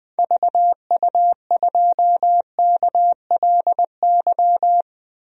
Morse code, heard as a keyed tone, about 20 wpm, 700 Hz. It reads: VU2KLY
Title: DXCC-100 Call Signs - 6 Characters - 20wpm